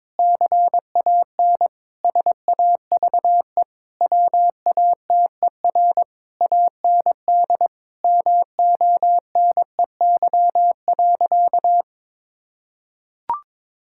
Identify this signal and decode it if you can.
Morse code, heard as a keyed tone, about 22 wpm, 700 Hz. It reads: CAN SAVE WATER AND MONEY.
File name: Difficult Sentences with Hints - 22wpm